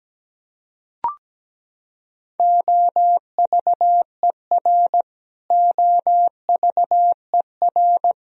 Morse code, heard as a keyed tone, about 17 wpm, 700 Hz. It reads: OVER OVER
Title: Top 100 Words Repeated 17wpm